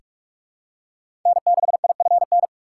Morse code, heard as a keyed tone, about 44 wpm, 690 Hz. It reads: N6IFN
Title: USA Call Signs - 5 Characters - 6x Character Spacing - 45wpm